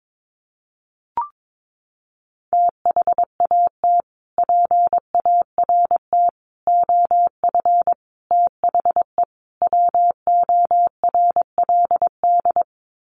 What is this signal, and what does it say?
Morse code, heard as a keyed tone, about 22 wpm, 700 Hz. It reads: THAT PART OF THE WORLD
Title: Sets of 5 Words Encore2 22wpm